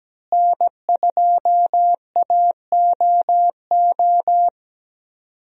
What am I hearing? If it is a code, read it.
Morse code, heard as a keyed tone, about 17 wpm, 700 Hz. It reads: N2AOO
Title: USA Call Signs - 5 Characters - Repeated 2x - 17wpm